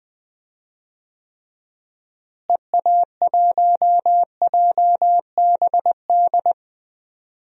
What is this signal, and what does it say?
Morse code, heard as a keyed tone, about 20 wpm, 700 Hz. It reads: EA1JBD
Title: DXCC-100 Call Signs - 6 Characters - Repeated 2x - 20wpm